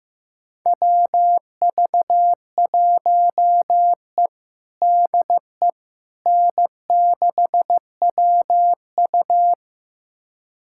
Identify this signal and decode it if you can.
Morse code, heard as a keyed tone, about 15 wpm, 700 Hz. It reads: WV1E DE N6WU
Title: USA Callsign DE Callsign - 4 Characters - 15wpm